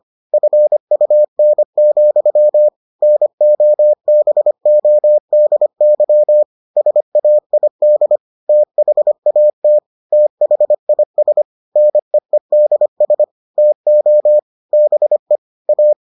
Morse code, heard as a keyed tone, about 25 wpm, 600 Hz. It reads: FUN, NOBODY SAID THAT THIS NEEDS TO BE A